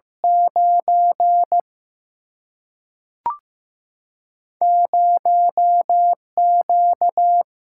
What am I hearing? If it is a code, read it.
Morse code, heard as a keyed tone, about 15 wpm, 700 Hz. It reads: 9 0Q
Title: Single-Letter Single-Number 15wpm